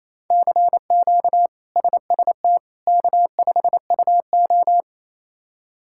Morse code, heard as a keyed tone, about 28 wpm, 700 Hz. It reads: CQ SST K5UO
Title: K1USN SST QSO Practice - 28wpm